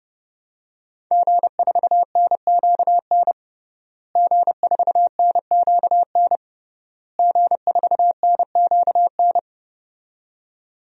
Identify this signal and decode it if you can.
Morse code, heard as a keyed tone, about 30 wpm, 700 Hz. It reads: G4NQN G4NQN G4NQN
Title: DXCC-100 Call Signs - 5 Characters - Repeated 3x - 30wpm